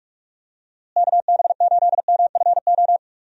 Morse code, heard as a keyed tone, about 46 wpm, 700 Hz. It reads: KB8MFO
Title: USA Call Signs - 6 Characters - Repeated 2x - 45wpm